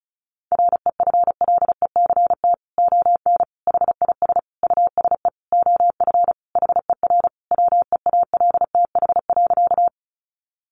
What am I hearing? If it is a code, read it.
Morse code, heard as a keyed tone, about 34 wpm, 700 Hz. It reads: REFLECT ON HIS USE OF HER WEALTH.
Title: Sentences from Top 1500 Words 35wpm